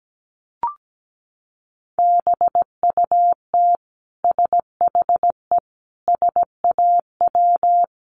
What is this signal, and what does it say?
Morse code, heard as a keyed tone, about 17 wpm, 700 Hz. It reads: BUT SHE SAW
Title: Sentences from Top 300 Words 17wpm